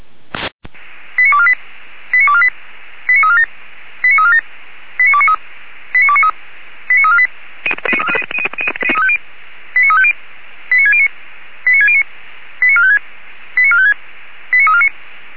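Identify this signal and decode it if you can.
Five-tone selective call (ZVEI), heard as quick runs of five tones, tones of about 70 ms. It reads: 89248 89248 89347 89347 89293 89293 89348 89350 89609 89609 89456 89456 89349
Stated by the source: ZVEI five-tone selective call 89332, received by SDR.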